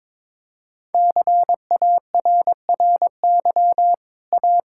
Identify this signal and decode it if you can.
Morse code, heard as a keyed tone, about 22 wpm, 700 Hz. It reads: CARRY A
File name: practice / 22wpm / Easy Sentences with Hints - 22wpm